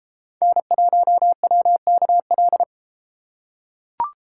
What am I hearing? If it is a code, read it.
Morse code, heard as a keyed tone, about 33 wpm, 700 Hz. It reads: N1WKL